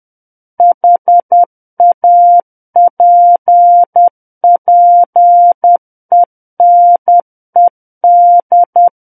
Morse code, heard as a keyed tone, about 10 wpm, 700 Hz. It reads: HAPPENED
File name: Book One / 01-Chapter 1 - 10 WPM